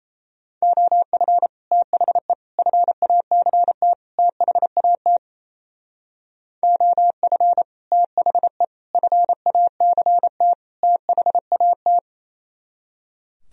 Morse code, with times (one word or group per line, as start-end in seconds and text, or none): 0.6-1.5 OF
1.7-2.3 THE
2.6-3.9 FACT
4.2-5.2 THAT
6.6-7.6 OF
7.9-8.6 THE
8.9-10.5 FACT
10.8-12.0 THAT